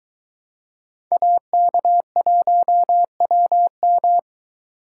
Morse code, heard as a keyed tone, about 22 wpm, 700 Hz. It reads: AK1WM